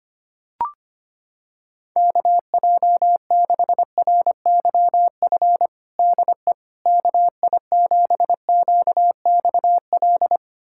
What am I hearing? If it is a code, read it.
Morse code, heard as a keyed tone, about 25 wpm, 700 Hz. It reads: KJ6RYF DE KI7QXL
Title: USA Callsign DE Callsign - 6 Characters - 25wpm